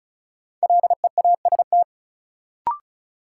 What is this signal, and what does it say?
Morse code, heard as a keyed tone, about 36 wpm, 700 Hz. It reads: LEAST
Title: Sets of 2 Words - New Word Drill - 35wpm